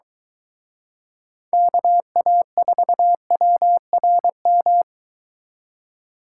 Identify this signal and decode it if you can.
Morse code, heard as a keyed tone, about 23 wpm, 700 Hz. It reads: KA4WRM